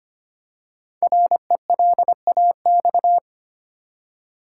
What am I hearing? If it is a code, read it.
Morse code, heard as a keyed tone, about 25 wpm, 700 Hz. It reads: RELAX
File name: 5 Letter Words - First 3 Letters Given - 25wpm